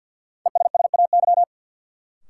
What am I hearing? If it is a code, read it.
Morse code, heard as a keyed tone, about 52 wpm, 700 Hz. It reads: ESSAY
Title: Sentences from Top 1500 Words - New Word Drill Alt - 50wpm